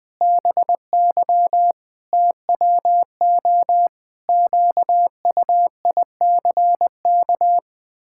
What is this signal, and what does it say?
Morse code, heard as a keyed tone, about 21 wpm, 700 Hz. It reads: BY TWO QUICK
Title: Pangram Challenge v2 - 20wpm